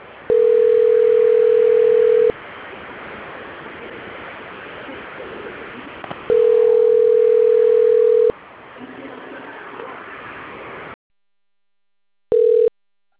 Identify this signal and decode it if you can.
Ringback tone (North America), 440 + 480 Hz.